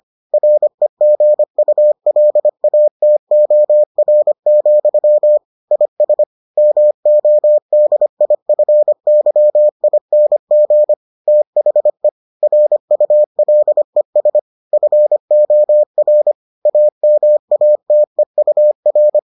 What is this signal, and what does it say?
Morse code, heard as a keyed tone, about 25 wpm, 600 Hz. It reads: REGULATOR, IS MODIFYING THE RULES FOR AMATEUR